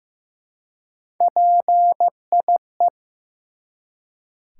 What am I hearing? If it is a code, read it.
Morse code, heard as a keyed tone, about 15 wpm, 700 Hz. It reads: PIE
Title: Three Letter Words - 15wpm